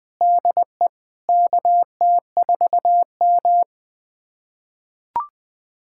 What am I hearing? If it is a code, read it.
Morse code, heard as a keyed tone, about 21 wpm, 700 Hz. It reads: DE KT4M
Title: USA Callsign DE Callsign - 4 Characters - 20wpm